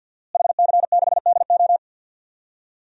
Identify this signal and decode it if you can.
Morse code, heard as a keyed tone, about 52 wpm, 700 Hz. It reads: HC6DO
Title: DXCC-100 Call Signs - 5 Characters - Repeated 3x - 50wpm